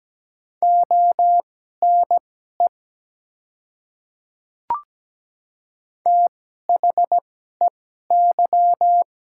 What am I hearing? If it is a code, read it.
Morse code, heard as a keyed tone, about 17 wpm, spaced as for 12 wpm, 700 Hz. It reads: ONE THEY
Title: 2x Character Spacing - Top 100 Words - 17wpm